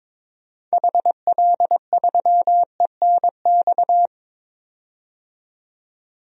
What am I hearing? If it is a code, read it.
Morse code, heard as a keyed tone, about 22 wpm, 700 Hz. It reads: HL3ENX
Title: DXCC-100 Call Signs - 6 Characters - 22wpm